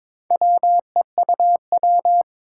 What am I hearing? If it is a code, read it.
Morse code, heard as a keyed tone, about 22 wpm, 700 Hz. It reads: WEUW
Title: Four Challenging Letter-Number Combinations 22wpm